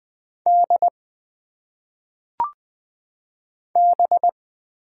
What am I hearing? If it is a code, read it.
Morse code, heard as a keyed tone, about 20 wpm, 700 Hz. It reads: D B
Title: Single Letters - Phonetic Pronunciation 20wpm